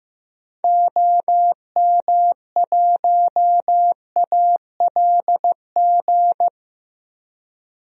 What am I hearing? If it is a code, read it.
Morse code, heard as a keyed tone, about 16 wpm, 700 Hz. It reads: OM1ALG
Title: DXCC-100 Call Signs - 6 Characters - Repeated 2x - 15wpm